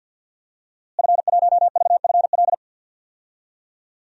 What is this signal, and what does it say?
Morse code, heard as a keyed tone, about 52 wpm, 700 Hz. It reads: F1VFL
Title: DXCC-100 Call Signs - 5 Characters - Repeated 3x - 50wpm